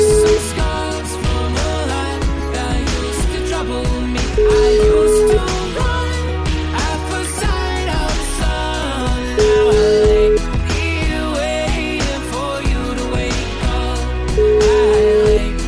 Ringback tone (Europe, 425 Hz), with noise.